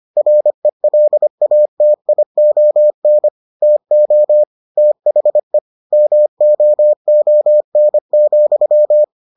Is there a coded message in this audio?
Yes: Morse code from 0.2 to 9.0 s, heard as a keyed tone, about 25 wpm, 600 Hz. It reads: RELATION TO THE MOON,